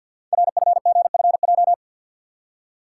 Morse code, heard as a keyed tone, about 50 wpm, 700 Hz. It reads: RFGFJ
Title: Code Groups - 5 Letters - 50wpm